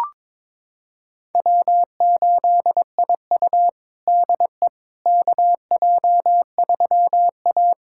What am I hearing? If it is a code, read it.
Morse code, heard as a keyed tone, about 22 wpm, 700 Hz. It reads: W8IU DE KJ3A
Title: USA Callsign DE Callsign - 4 Characters - 22wpm